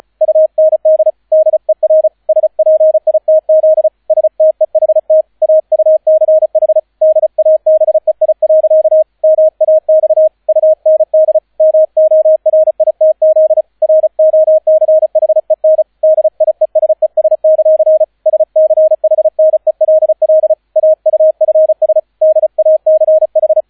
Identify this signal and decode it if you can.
Morse code, heard as a keyed tone, about 36 wpm, 630 Hz, with radio noise. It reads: UND DER SPITZ STEHT AUCH DABEI. MAX UND MORITZ ROCHEN DIESES; SCHNELL AUFS DACH